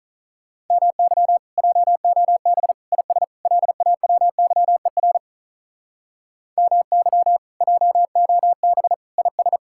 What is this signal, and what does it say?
Morse code, heard as a keyed tone, about 46 wpm, 700 Hz. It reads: MY JOB IS LAWYER MY JOB IS